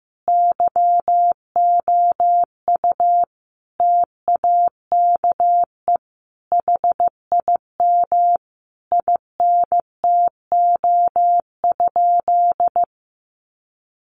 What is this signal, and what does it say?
Morse code, heard as a keyed tone, about 14 wpm, 700 Hz. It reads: YOU TAKE HIM INTO?
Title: Sentences from Top 600 Words 15wpm